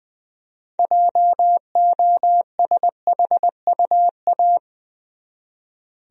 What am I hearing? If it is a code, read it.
Morse code, heard as a keyed tone, about 20 wpm, 700 Hz. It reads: JOSHUA
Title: USA Common Names - 20wpm